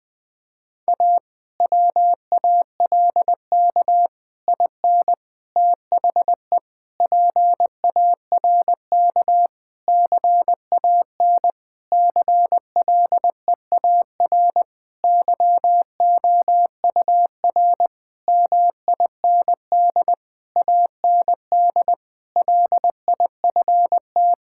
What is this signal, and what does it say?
Morse code, heard as a keyed tone, about 20 wpm, 700 Hz. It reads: A WALK IN THE PARK CAN CLEAR YOUR MIND AND LIFT
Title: Difficult Sentences with Hints - 20wpm